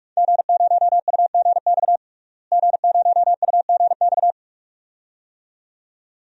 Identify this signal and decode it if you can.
Morse code, heard as a keyed tone, about 46 wpm, 700 Hz. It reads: G0UGX G0UGX
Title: DXCC-100 Call Signs - 5 Characters - Repeated 2x - 45wpm